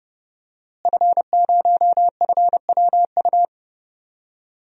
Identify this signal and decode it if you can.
Morse code, heard as a keyed tone, about 30 wpm, 700 Hz. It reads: F0FWU
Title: DXCC-100 Call Signs - 5 Characters - Repeated 2x - 30wpm